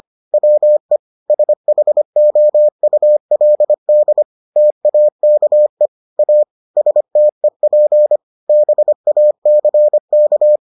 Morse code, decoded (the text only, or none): WE SHOULD TAKE A STEP BACK